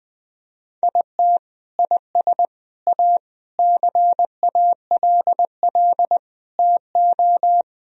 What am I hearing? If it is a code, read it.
Morse code, heard as a keyed tone, about 20 wpm, 700 Hz. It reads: IT IS A CALL TO
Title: Sentences from Top 1000 Words - Review Part 4 - 20wpm